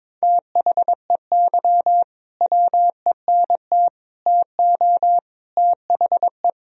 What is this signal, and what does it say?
Morse code, heard as a keyed tone, about 22 wpm, 700 Hz. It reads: THEY WENT TO THE
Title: Sentences from Top 1000 Words - Review Part 4 - 22wpm